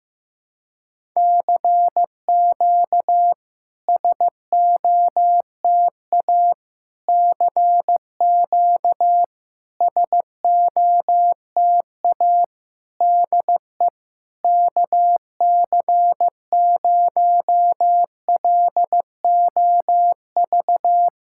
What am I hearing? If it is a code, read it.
Morse code, heard as a keyed tone, about 15 wpm, 700 Hz. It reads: CQ SOTA CQ SOTA DE KC0LOV
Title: SOTA QSO Practice 15wpm